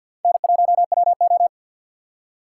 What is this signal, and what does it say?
Morse code, heard as a keyed tone, about 66 wpm, spaced as for 58 wpm, 700 Hz. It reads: N1WO
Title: Speed Racing - USA 1x2 Call Signs 50wpm